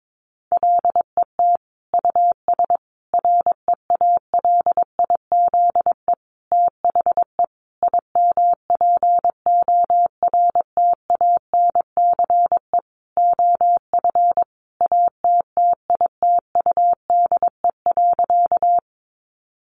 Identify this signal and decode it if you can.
Morse code, heard as a keyed tone, about 22 wpm, 700 Hz. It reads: LET US REALIZE THE IMPORTANCE OF ATTITUDE.
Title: Sentences from Top 1300 Words 22wpm